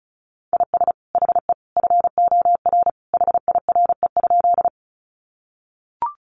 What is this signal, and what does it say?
Morse code, heard as a keyed tone, about 34 wpm, 700 Hz. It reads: IS HE FOR HIRE?
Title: Sentences from Top 1400 Words 35wpm